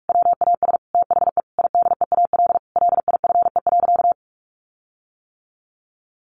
Morse code, heard as a keyed tone, about 44 wpm, 700 Hz. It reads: WAS THE IDEAL LIFE.
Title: Sentences from Top 1400 Words 45wpm